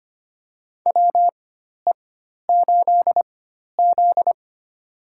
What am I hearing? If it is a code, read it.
Morse code, heard as a keyed tone, about 25 wpm, spaced as for 12 wpm, 700 Hz. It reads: WE8Z
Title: USA Call Signs - 4 Characters - 4x Character Spacing - 25wpm